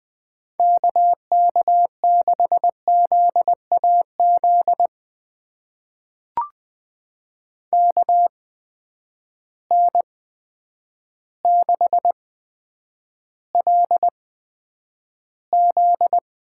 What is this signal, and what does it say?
Morse code, heard as a keyed tone, about 20 wpm, spaced as for 9 wpm, 700 Hz. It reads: KK6ZAZ KN6LZ
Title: USA Call Signs - 8x Character Spacing - 20wpm